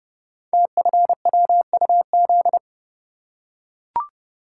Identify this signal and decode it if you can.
Morse code, heard as a keyed tone, about 29 wpm, 700 Hz. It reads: TFWUZ
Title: Code Groups - 5 Letter:Numbers - 30wpm